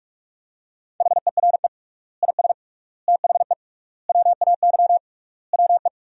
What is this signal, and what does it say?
Morse code, heard as a keyed tone, about 43 wpm, 700 Hz. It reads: HERE IS THE WAY WE
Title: 3x Word-Spacing - Sentences from Top 200 Words - 45wpm